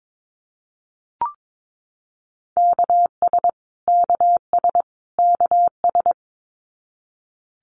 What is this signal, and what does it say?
Morse code, heard as a keyed tone, about 22 wpm, 700 Hz. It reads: KS KS KS
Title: US States Repeated 3x 22wpm